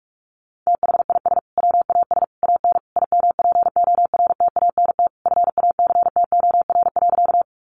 Morse code, heard as a keyed tone, about 44 wpm, 700 Hz. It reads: THIS WAS AN IMPORTANT FACTOR.
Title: Sentences from Top 1300 Words 45wpm